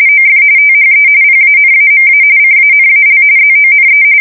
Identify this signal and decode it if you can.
RTTY, about 45.45 baud. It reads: OF THE PANAMA CANAL ZONE
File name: fsk170 / usnavy3